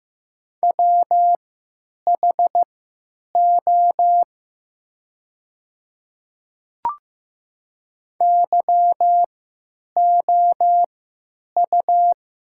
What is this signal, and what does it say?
Morse code, heard as a keyed tone, about 15 wpm, spaced as for 9 wpm, 700 Hz. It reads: WHO YOU
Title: 3x Character Spacing - Top 100 Words - 15wpm